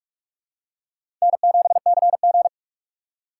Morse code, heard as a keyed tone, about 46 wpm, 700 Hz. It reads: N7CG
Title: USA Call Signs - 4 Characters - 4x Character Spacing - 45wpm